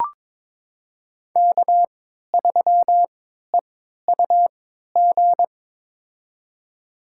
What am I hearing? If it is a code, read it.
Morse code, heard as a keyed tone, about 21 wpm, spaced as for 12 wpm, 700 Hz. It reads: K3EUG
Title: USA Call Signs - 5 Characters - 3x Character Spacing - 22wpm